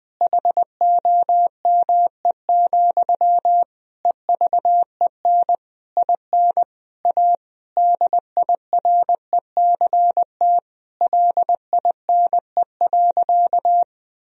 Morse code, heard as a keyed tone, about 21 wpm, 700 Hz. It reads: HOME, EVEN IN A DIRECT LINE.
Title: Sentences from Top 1000 Words - Review Part 2 - 20wpm